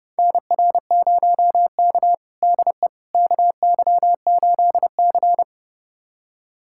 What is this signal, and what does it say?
Morse code, heard as a keyed tone, about 30 wpm, 700 Hz. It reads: NR0K DE KY8C
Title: USA Callsign DE Callsign - 4 Characters - 30wpm